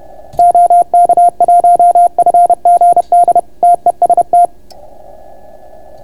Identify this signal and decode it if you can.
Morse code, heard as a keyed tone, about 30 wpm, 690 Hz. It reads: OK1FGD TEST